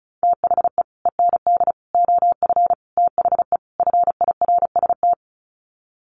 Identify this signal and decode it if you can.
Morse code, heard as a keyed tone, about 34 wpm, 700 Hz. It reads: THE END OF THE FIRST
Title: Sets of 5 Words Encore 35wpm